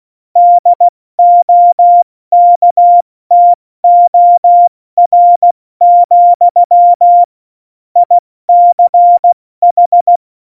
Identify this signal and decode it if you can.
Morse code, heard as a keyed tone, about 16 wpm, 700 Hz. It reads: DOKTOR, ICH
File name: Morsekurs / Witze / Witz 05 Mai, 2020 70 F80